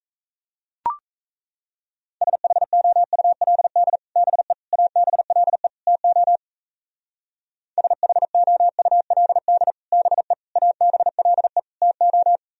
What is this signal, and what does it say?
Morse code, heard as a keyed tone, about 35 wpm, 700 Hz. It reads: SHOULD BE ABLE TO SHOULD BE ABLE TO